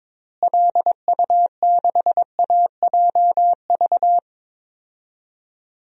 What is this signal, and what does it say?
Morse code, heard as a keyed tone, about 22 wpm, 700 Hz. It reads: LU6AJV
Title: DXCC-100 Call Signs - 6 Characters - Repeated 2x - 22wpm